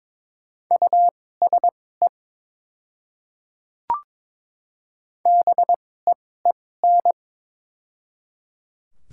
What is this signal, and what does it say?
Morse code, heard as a keyed tone, about 23 wpm, spaced as for 16 wpm, 700 Hz. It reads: USE BEEN
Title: 2x Character Spacing - Top 100 Words - 22wpm